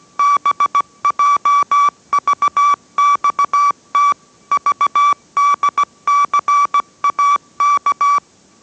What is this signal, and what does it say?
Morse code, heard as a keyed tone, about 17 wpm, 1150 Hz. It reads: BJVXT VDCAK